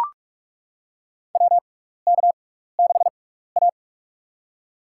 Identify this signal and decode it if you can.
Morse code, heard as a keyed tone, about 42 wpm, spaced as for 15 wpm, 700 Hz. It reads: WK6A
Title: USA Call Signs - 6x Character Spacing - 45wpm